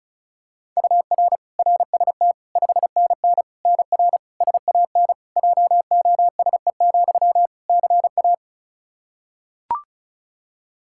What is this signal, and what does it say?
Morse code, heard as a keyed tone, about 36 wpm, 700 Hz. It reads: UR RST 5NN NR SAN JOSE, CA
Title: Practice QSO - Exchange 1 - Beginning - 35wpm